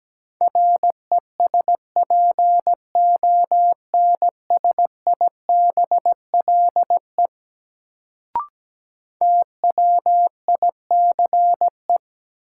Morse code, heard as a keyed tone, about 17 wpm, 700 Hz. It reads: RESPONSIBLE TWICE
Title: Sentences from Top 1600 Words - New Word Drill Alt - 17wpm